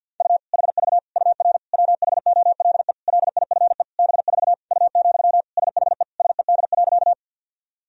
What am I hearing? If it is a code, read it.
Morse code, heard as a keyed tone, about 53 wpm, 700 Hz. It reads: U HV UR WHOLE LIFE B4 U, SHE SED.
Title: Sentences from Top 400 Words and Abbreviations 50wpm